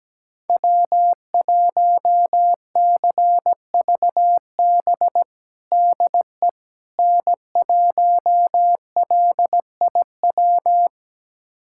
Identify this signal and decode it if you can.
Morse code, heard as a keyed tone, about 17 wpm, 700 Hz. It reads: W1CVB DE N1LIW
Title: USA Callsign DE Callsign - 5 Characters - 17wpm